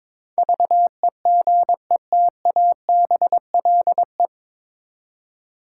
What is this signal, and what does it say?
Morse code, heard as a keyed tone, about 22 wpm, 700 Hz. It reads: VEGETABLE
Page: Morse Code Ninja practice 22wpm